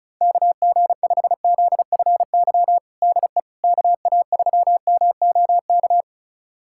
Morse code, heard as a keyed tone, about 36 wpm, 700 Hz. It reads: KG5ZFY DE KA3MOK
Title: USA Callsign DE Callsign - 6 Characters - 35wpm